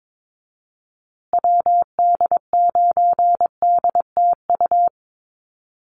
Morse code, heard as a keyed tone, about 22 wpm, 700 Hz. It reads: WD9DTU
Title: USA Call Signs Repeated 3x 22wpm